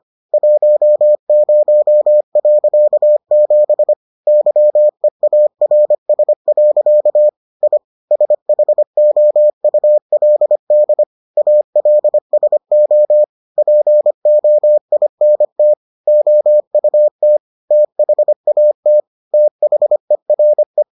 Morse code, heard as a keyed tone, about 25 wpm, 600 Hz. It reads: 10.7 YEARS. I SHOULD ALSO POINT OUT THAT THERE